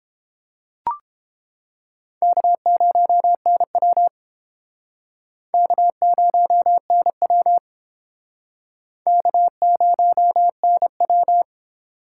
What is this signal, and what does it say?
Morse code, heard as a keyed tone, about 30 wpm, 700 Hz. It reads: K0NW K0NW K0NW